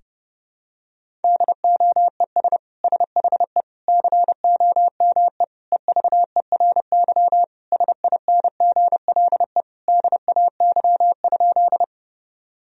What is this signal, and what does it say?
Morse code, heard as a keyed tone, about 32 wpm, 700 Hz. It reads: DOES SHE COME EVERY SINGLE DAY?
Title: Sentences from Top 1000 Words - Review Part 4 - 30wpm